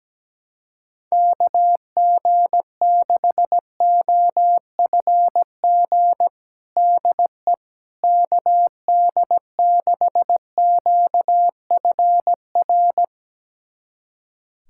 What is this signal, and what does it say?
Morse code, heard as a keyed tone, about 17 wpm, 700 Hz. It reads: KG6OFG DE KD6QFR
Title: USA Callsign DE Callsign 17wpm